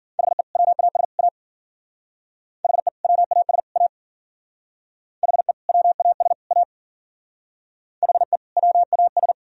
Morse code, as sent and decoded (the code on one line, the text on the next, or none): .... / .-- .- ... / .- / .... / .-- .- ... / .- / .... / .-- .- ... / .- / .... / .-- .- ...
HE WAS A HE WAS A HE WAS A HE WAS